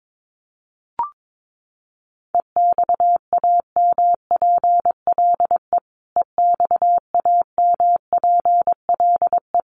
Morse code, heard as a keyed tone, about 22 wpm, 700 Hz. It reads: EXAMPLE EXAMPLE
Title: Top 500 Words Repeated 22wpm